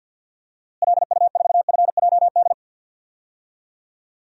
Morse code, heard as a keyed tone, about 50 wpm, 700 Hz. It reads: LU4FJD
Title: DXCC-100 Call Signs - 6 Characters - Repeated 3x - 50wpm